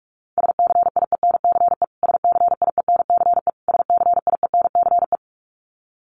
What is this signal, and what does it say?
Morse code, heard as a keyed tone, about 44 wpm, 700 Hz. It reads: SCIENCE SCIENCE SCIENCE
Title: Top 500 Words Repeated 45wpm